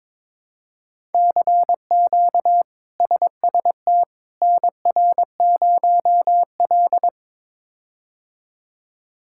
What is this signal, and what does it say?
Morse code, heard as a keyed tone, about 22 wpm, 700 Hz. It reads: CQ SST NR0L
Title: K1USN SST QSO Practice - 22wpm